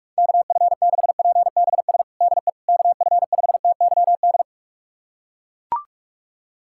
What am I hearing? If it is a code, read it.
Morse code, heard as a keyed tone, about 41 wpm, 700 Hz. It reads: KF6PBS DE KF5TYD
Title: USA Callsign DE Callsign - 6 Characters - 45wpm